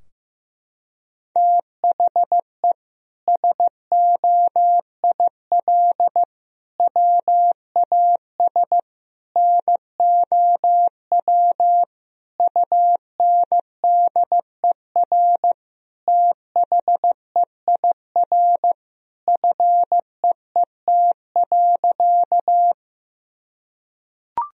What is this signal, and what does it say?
Morse code, heard as a keyed tone, about 16 wpm, 700 Hz. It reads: THE SOIL WAS NOW UNDER THEIR FEET.
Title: Sentences from Top 1000 Words - Review Part 4 - 15wpm